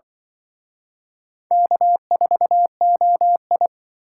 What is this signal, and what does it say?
Morse code, heard as a keyed tone, about 25 wpm, 700 Hz. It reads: K4OI
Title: Speed Racing - USA 1x2 Call Signs 20wpm